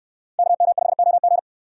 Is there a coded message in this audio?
Yes: Morse code from 0.4 to 1.4 s, heard as a keyed tone, about 67 wpm, 700 Hz. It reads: KM5OG